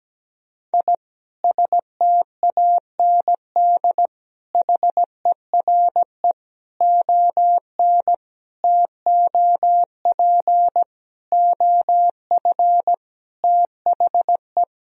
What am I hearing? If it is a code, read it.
Morse code, heard as a keyed tone, about 17 wpm, 700 Hz. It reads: I STAND HERE ON TOP OF THE
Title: Sentences from Top 1000 Words - Review Part 1 - 17wpm